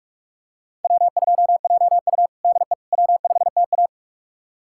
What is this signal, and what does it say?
Morse code, heard as a keyed tone, about 47 wpm, 690 Hz. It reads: W2JU DE W5TA
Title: USA Callsign DE Callsign - 4 Characters - 45wpm